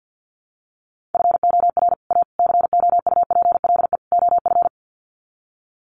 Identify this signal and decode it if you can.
Morse code, heard as a keyed tone, about 50 wpm, 700 Hz. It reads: FOR A COUPLE OF
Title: Sets of 4 Words 50wpm